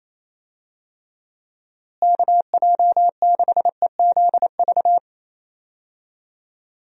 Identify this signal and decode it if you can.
Morse code, heard as a keyed tone, about 28 wpm, 700 Hz. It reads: KJ6EZV
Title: USA Call Signs - 6 Characters - Repeated 2x - 28wpm